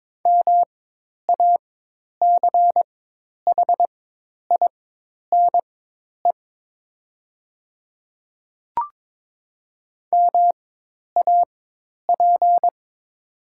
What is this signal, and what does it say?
Morse code, heard as a keyed tone, about 22 wpm, spaced as for 10 wpm, 700 Hz. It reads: MACHINE MAP
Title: 4x Character Spacing - Top 500 Words - 22wpm